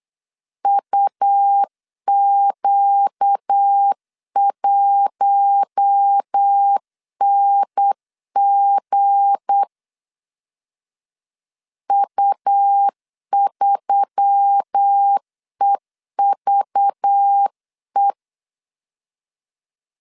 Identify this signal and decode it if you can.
Morse code, heard as a keyed tone, about 8 wpm, 800 Hz. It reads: UQ1NG U3EVE
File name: CW / DET / 1 9 1500C S B 8wpm 5624